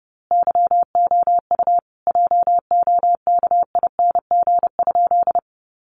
Morse code, heard as a keyed tone, about 30 wpm, 700 Hz. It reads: YOU JOKING?
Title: A Sound Of Thunder 30wpm